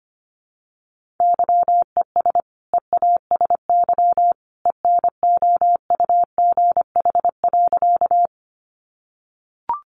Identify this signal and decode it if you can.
Morse code, heard as a keyed tone, about 25 wpm, 700 Hz. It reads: YES EASY ENOUGH.
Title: Sentences from Top 700 Words 25wpm